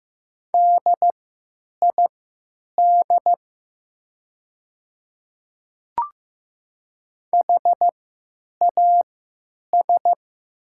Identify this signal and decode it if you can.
Morse code, heard as a keyed tone, about 15 wpm, spaced as for 9 wpm, 700 Hz. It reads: DID HAS
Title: 3x Character Spacing - Top 100 Words - 15wpm